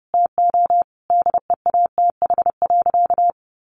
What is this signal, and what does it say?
Morse code, heard as a keyed tone, about 30 wpm, 700 Hz. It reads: TO DEATH.
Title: Sentences from Top 1400 Words 30wpm